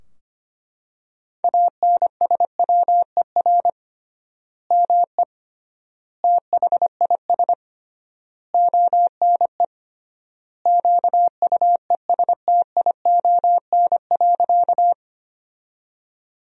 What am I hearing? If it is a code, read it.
Morse code, heard as a keyed tone, about 25 wpm, 700 Hz. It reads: ANSWER ME THIS ONE QUESTION.
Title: 3x Word-Spacing - Sentences from Top 400 Words - 25wpm